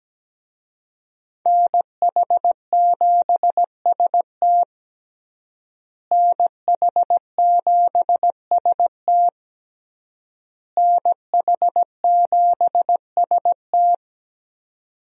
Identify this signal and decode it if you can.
Morse code, heard as a keyed tone, about 17 wpm, 700 Hz. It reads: NH7ST NH7ST NH7ST
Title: USA Call Signs - 5 Characters - Repeated 3x - 17wpm